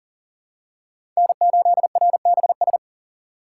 Morse code, heard as a keyed tone, about 40 wpm, 700 Hz. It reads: N8RBS
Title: USA Call Signs - 5 Characters - 3x Character Spacing - 40wpm